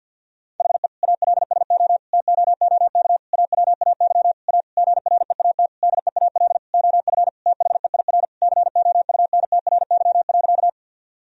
Morse code, heard as a keyed tone, about 50 wpm, 700 Hz. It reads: HE ALSO TOOK AWAY A GREAT DEAL OF THEIR COUNTRY.
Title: Sentences from Top 1000 Words - Review Part 1 - 50wpm